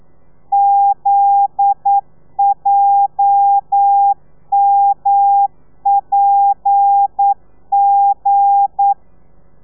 Morse code, heard as a keyed tone, about 9 wpm, 800 Hz. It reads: ZJMPG
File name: telegrafia / i2tfj / Lez9 40